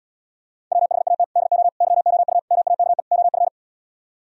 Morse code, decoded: WHAT DO YOU MAKE OF

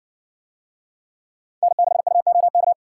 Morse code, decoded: N6UOK

